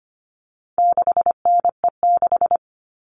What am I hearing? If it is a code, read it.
Morse code, heard as a keyed tone, about 25 wpm, 700 Hz. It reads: 6NE6
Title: All Cut Numbers - 4-Digit 25wpm